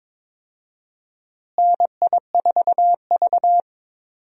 Morse code, heard as a keyed tone, about 22 wpm, 700 Hz. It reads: NI4V
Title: USA Call Signs - 4 Characters - Repeated 2x - 22wpm